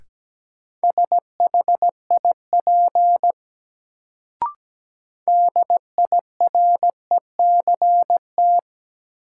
Morse code, heard as a keyed tone, about 17 wpm, 700 Hz. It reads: SHIP DIRECT
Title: Sentences from Top 400 Words - New Word Drill - 17wpm